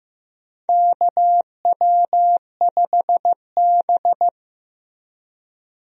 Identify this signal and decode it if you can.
Morse code, heard as a keyed tone, about 15 wpm, 700 Hz. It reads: KW5B
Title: USA Call Signs - 4 Characters - Repeated 3x - 15wpm